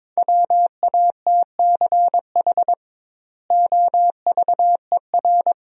Morse code, heard as a keyed tone, about 22 wpm, 700 Hz. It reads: WATCH OVER
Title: 2x Word-Spacing - Sentences from Top 300 Words - 22wpm